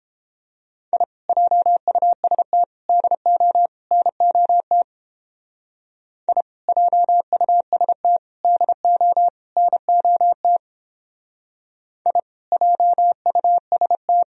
Morse code, heard as a keyed tone, about 30 wpm, 700 Hz. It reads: I JUST DO NOT I JUST DO NOT I JUST